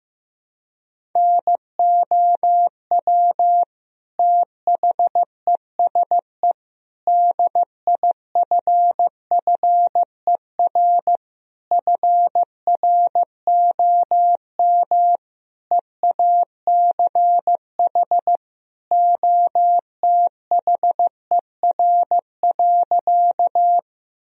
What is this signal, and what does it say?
Morse code, heard as a keyed tone, about 15 wpm, 700 Hz. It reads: NOW THESE DIFFER FROM EACH OTHER.
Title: Sentences from Top 1000 Words - Review Part 1 - 15wpm